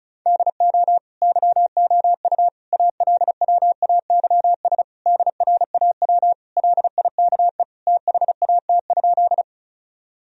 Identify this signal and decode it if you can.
Morse code, heard as a keyed tone, about 34 wpm, 700 Hz. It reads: DO YOU ALWAYS DRAW LIKE THAT?
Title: Sentences from Top 1000 Words - Review Part 1 - 35wpm